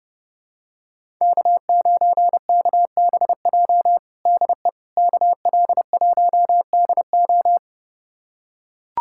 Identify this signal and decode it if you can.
Morse code, heard as a keyed tone, about 32 wpm, 700 Hz. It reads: K9KBJ DE KL1DO
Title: USA Callsign DE Callsign - 5 Characters - 30wpm